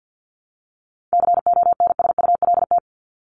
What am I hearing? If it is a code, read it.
Morse code, heard as a keyed tone, about 50 wpm, 700 Hz. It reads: CONSULT
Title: Top 1751-2000 Words 50wpm